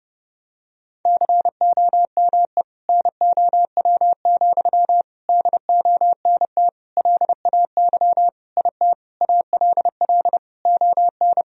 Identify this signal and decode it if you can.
Morse code, heard as a keyed tone, about 30 wpm, 700 Hz. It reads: COME NOW, DONT LAY IT ALL ON
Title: Sentences from Top 1000 Words - Review Part 2 - 30wpm